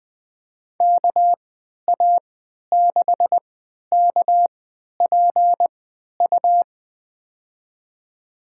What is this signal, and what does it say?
Morse code, heard as a keyed tone, about 20 wpm, spaced as for 11 wpm, 700 Hz. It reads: KA6KPU
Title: USA Call Signs - 6 Characters - 3x Character Spacing - 20wpm